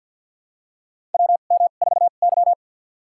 Morse code, heard as a keyed tone, about 52 wpm, spaced as for 36 wpm, 700 Hz. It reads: WM4Y
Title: USA Call Signs - 4 Characters - 2x Character Spacing - 50wpm